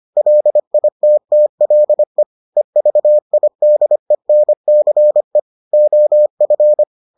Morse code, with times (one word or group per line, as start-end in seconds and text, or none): 0.2-2.2 LITTLE
2.6-5.4 EVIDENCE
5.7-6.8 OF